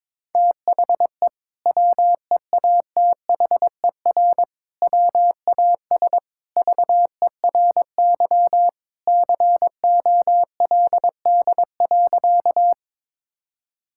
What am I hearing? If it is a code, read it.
Morse code, heard as a keyed tone, about 22 wpm, 700 Hz. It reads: THE WEATHER WAS VERY COLD.
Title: Sentences from Top 1000 Words - Review Part 5 - 22wpm